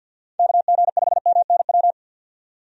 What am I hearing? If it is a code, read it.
Morse code, heard as a keyed tone, about 50 wpm, 700 Hz. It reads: KG5MNW
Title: USA Call Signs - 6 Characters - 50wpm